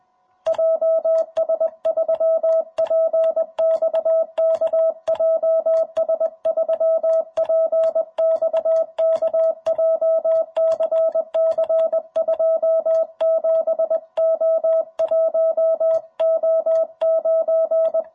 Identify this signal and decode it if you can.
Morse code, heard as a keyed tone, about 21 wpm, 660 Hz. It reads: JS3PXK JS3PXK JCC 27O1O9
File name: jh8cbh / 2021cw